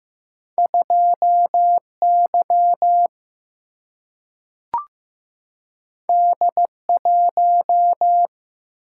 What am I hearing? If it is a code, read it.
Morse code, heard as a keyed tone, about 15 wpm, 700 Hz. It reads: 2 Y D 1
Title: Single-Letter Single-Number 15wpm